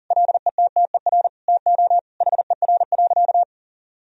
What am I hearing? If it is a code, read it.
Morse code, heard as a keyed tone, about 40 wpm, 700 Hz. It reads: LETTER TO HER.
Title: Sentences from Top 1000 Words - Review Part 3 - 40wpm